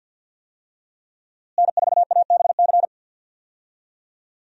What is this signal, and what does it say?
Morse code, heard as a keyed tone, about 53 wpm, 700 Hz. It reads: N4ABC
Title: USA Call Signs - 5 Characters - Repeated 3x - 50wpm